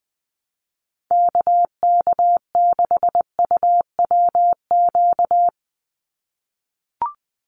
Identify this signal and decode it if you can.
Morse code, heard as a keyed tone, about 20 wpm, 700 Hz. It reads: KK6UWQ
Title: USA Call Signs 20wpm